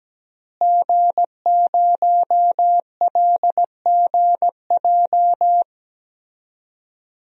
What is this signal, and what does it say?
Morse code, heard as a keyed tone, about 17 wpm, 700 Hz. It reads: G0LGJ
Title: DXCC-100 Call Signs - 5 Characters - Repeated 2x - 17wpm